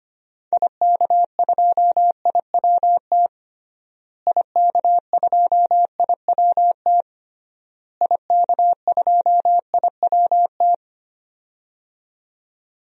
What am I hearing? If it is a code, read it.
Morse code, heard as a keyed tone, about 25 wpm, 700 Hz. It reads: IK2IWT IK2IWT IK2IWT
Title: DXCC-100 Call Signs - 6 Characters - Repeated 3x - 25wpm